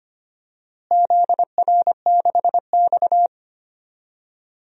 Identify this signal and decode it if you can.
Morse code, heard as a keyed tone, about 25 wpm, 700 Hz. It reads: ZR6X
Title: DXCC-100 Call Signs - 4 Characters - Repeated 3x - 25wpm